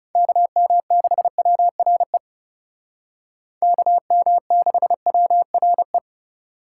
Morse code, heard as a keyed tone, about 37 wpm, 700 Hz. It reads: KM6WRE KM6WRE
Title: Speed Racing - USA 2x3 Call Signs 30wpm